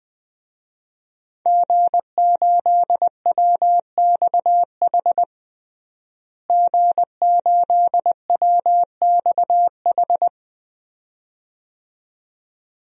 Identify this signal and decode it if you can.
Morse code, heard as a keyed tone, about 20 wpm, 700 Hz. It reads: G8WXH G8WXH
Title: DXCC-100 Call Signs - 5 Characters - Repeated 2x - 20wpm